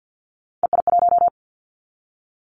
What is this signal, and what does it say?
Morse code, heard as a keyed tone, about 50 wpm, 700 Hz. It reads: EI1
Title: Callsign Prefixes for Top 50 DXCC - 50wpm